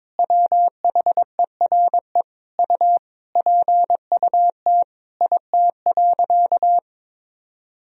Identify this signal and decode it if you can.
Morse code, heard as a keyed tone, about 22 wpm, 700 Hz. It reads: WHERE U PUT IT.
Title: Sentences from Top 200 Words and Abbreviations 22wpm